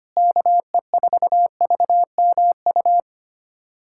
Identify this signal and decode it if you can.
Morse code, heard as a keyed tone, about 25 wpm, 700 Hz. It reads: KE4VMU
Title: USA Callsign DE Callsign - 6 Characters - 25wpm